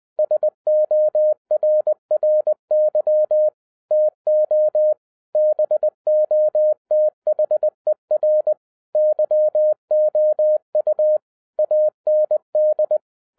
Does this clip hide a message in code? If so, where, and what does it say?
Yes: Morse code from 0.2 to 13.0 s, heard as a keyed tone, about 20 wpm, 610 Hz. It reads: SORRY TO BOTHER YOU AND